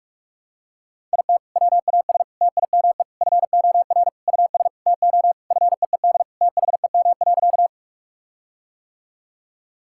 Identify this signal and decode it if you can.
Morse code, heard as a keyed tone, about 41 wpm, 700 Hz. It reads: IT WAS TIME FOR US TO FEED THEM.
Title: Sentences from Top 1600 Words 45wpm